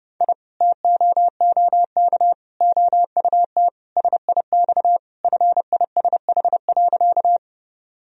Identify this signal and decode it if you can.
Morse code, heard as a keyed tone, about 32 wpm, 700 Hz. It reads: I TOOK OUT SIX FISH.
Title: Sentences from Top 1000 Words - Review Part 2 - 30wpm